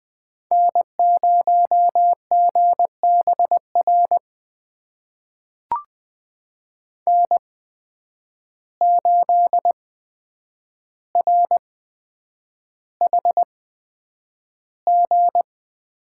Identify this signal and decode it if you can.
Morse code, heard as a keyed tone, about 20 wpm, spaced as for 9 wpm, 700 Hz. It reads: N0GBR N8RHG
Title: USA Call Signs - 5 Characters - 8x Character Spacing - 20wpm